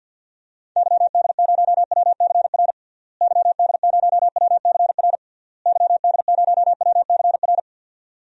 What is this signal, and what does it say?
Morse code, heard as a keyed tone, about 50 wpm, 700 Hz. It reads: YD0WCR YD0WCR YD0WCR
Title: DXCC-100 Call Signs - 6 Characters - Repeated 3x - 50wpm